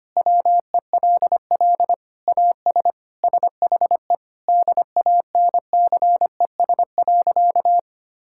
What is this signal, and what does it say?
Morse code, heard as a keyed tone, about 25 wpm, 700 Hz. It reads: WELL AS SHE DANCES.
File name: practice / 25wpm / Sentences with 3-Word Phrases - 25wpm